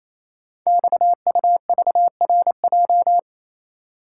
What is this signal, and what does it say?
Morse code, heard as a keyed tone, about 29 wpm, 700 Hz. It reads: XUVRJ
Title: Code Groups - 5 Letters - 28wpm